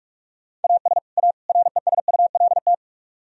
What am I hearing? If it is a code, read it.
Morse code, heard as a keyed tone, about 46 wpm, 700 Hz. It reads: AS A RESULT